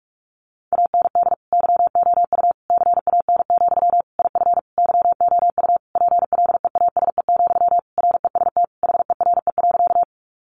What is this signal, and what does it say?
Morse code, heard as a keyed tone, about 44 wpm, 700 Hz. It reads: AND YOU CAN, IF YOU PLEASE, REST HERE.
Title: Sentences from Top 1500 Words 45wpm